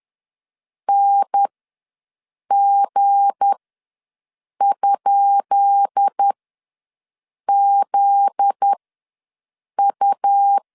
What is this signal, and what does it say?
Morse code, heard as a keyed tone, about 10 wpm, spaced as for 6 wpm, 800 Hz. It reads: NG?ZU
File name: LEZIONI / 10 SIGNS / 10B INT ws 10wpm 4134